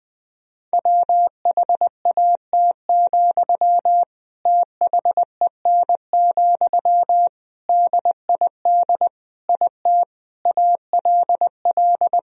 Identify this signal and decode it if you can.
Morse code, heard as a keyed tone, about 20 wpm, 700 Hz. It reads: WHAT, THEN, DID IT ALL
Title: Sentences from Top 1000 Words - Review Part 4 - 20wpm